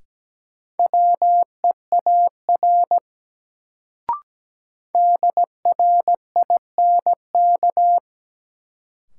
Morse code, heard as a keyed tone, about 17 wpm, 700 Hz. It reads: WEAR DRINK
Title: Sentences from Top 1500 Words - New Word Drill Alt - 17wpm